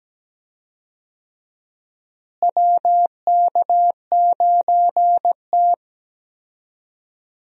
Morse code, heard as a keyed tone, about 17 wpm, 700 Hz. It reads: WK9T
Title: USA Call Signs - 4 Characters - Repeated 2x - 17wpm